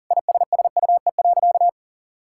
Morse code, heard as a keyed tone, about 40 wpm, 700 Hz. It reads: ISSUE.
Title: Sentences from Top 1000 Words - Review Part 6 - 40wpm